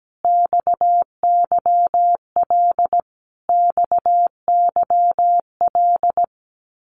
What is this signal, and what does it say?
Morse code, heard as a keyed tone, about 17 wpm, 700 Hz. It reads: XYL XYL
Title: Abbreviations Repeated 3x - 17wpm